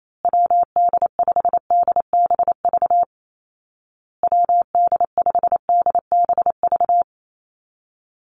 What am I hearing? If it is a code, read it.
Morse code, heard as a keyed tone, about 28 wpm, 700 Hz. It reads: WD5DBV WD5DBV
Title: Call Signs Repeated 2x - 28wpm